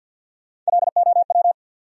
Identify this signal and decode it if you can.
Morse code, heard as a keyed tone, about 67 wpm, spaced as for 58 wpm, 700 Hz. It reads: LOW
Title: Speed-Racing Top 300 Words 50wpm